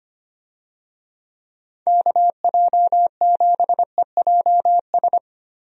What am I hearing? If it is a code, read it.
Morse code, heard as a keyed tone, about 25 wpm, 700 Hz. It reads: KJ7EJS
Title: USA Call Signs - 6 Characters - 25wpm